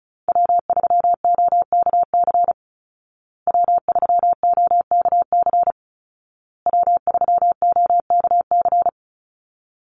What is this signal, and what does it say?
Morse code, heard as a keyed tone, about 34 wpm, 700 Hz. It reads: W3OKC W3OKC W3OKC
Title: Call Signs Repeated 2x - 35wpm